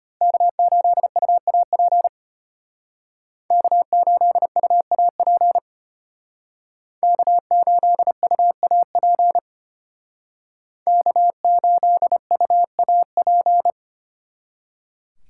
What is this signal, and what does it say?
Morse code, heard as a keyed tone, about 33 wpm, 700 Hz. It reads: K8UAP K8UAP K8UAP K8UAP